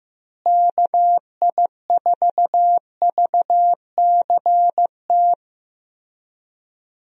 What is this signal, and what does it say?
Morse code, heard as a keyed tone, about 15 wpm, 700 Hz. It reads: KI4VCT
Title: USA Call Signs - 6 Characters - Repeated 3x - 15wpm